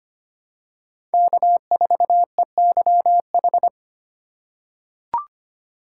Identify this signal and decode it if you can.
Morse code, heard as a keyed tone, about 25 wpm, 700 Hz. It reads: K4EYH
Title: USA Call Signs - 5 Characters - 4x Character Spacing - 25wpm